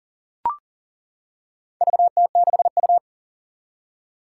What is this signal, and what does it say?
Morse code, heard as a keyed tone, about 39 wpm, 700 Hz. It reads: VT6U
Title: Four Challenging Letter-Number Combinations 40wpm